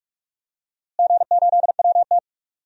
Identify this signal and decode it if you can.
Morse code, heard as a keyed tone, about 41 wpm, 700 Hz. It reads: G8WT
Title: DXCC-100 Call Signs - 4 Characters - Repeated 2x - 45wpm